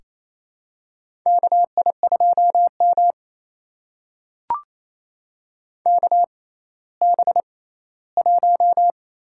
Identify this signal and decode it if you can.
Morse code, heard as a keyed tone, about 28 wpm, spaced as for 16 wpm, 700 Hz. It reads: KI2M KB1
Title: USA Call Signs - 6x Character Spacing - 28wpm